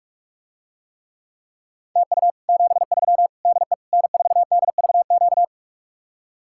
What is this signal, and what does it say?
Morse code, heard as a keyed tone, about 49 wpm, 700 Hz. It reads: TU 73 DE N4DVQ